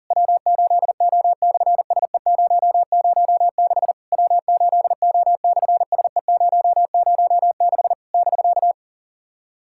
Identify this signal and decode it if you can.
Morse code, heard as a keyed tone, about 44 wpm, 700 Hz. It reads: W8O/SE006 W8O/SE006 <BK>
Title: SOTA QSO Practice 40wpm